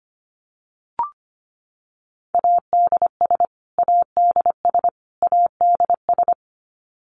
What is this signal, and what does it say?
Morse code, heard as a keyed tone, about 25 wpm, 700 Hz. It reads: ADS ADS ADS
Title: Abbreviations Repeated 3x - 25wpm